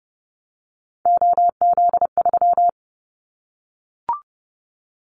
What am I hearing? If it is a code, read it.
Morse code, heard as a keyed tone, about 30 wpm, 700 Hz. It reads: OZ3
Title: Callsign Prefixes for Top 50 DXCC - 30wpm